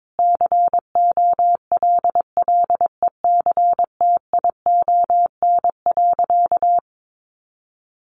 Morse code, heard as a keyed tone, about 22 wpm, 700 Hz. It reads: COLLECTION.
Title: Sentences from Top 1200 Words 22wpm